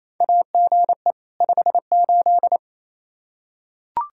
Morse code, heard as a keyed tone, about 28 wpm, 700 Hz. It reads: AGE 58
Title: Practice QSO - Exchange 3 - More Info - 28wpm